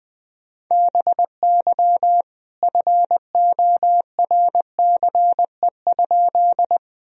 Morse code, heard as a keyed tone, about 20 wpm, 700 Hz. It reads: BY FORCE?
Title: Sentences from Top 1000 Words - Review Part 3 - 20wpm